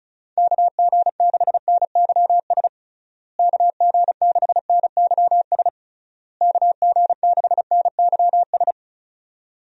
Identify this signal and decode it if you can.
Morse code, heard as a keyed tone, about 34 wpm, 700 Hz. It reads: KG6NYS KG6NYS KG6NYS
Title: USA Call Signs - 6 Characters - Repeated 3x - 35wpm